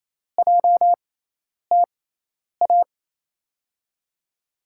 Morse code, heard as a keyed tone, about 28 wpm, spaced as for 10 wpm, 700 Hz. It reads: JTA
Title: USA Call Signs - 6x Character Spacing - 28wpm